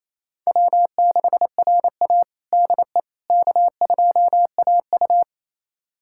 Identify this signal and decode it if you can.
Morse code, heard as a keyed tone, about 28 wpm, 700 Hz. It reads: W6RA DE K2AU
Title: USA Callsign DE Callsign - 4 Characters - 28wpm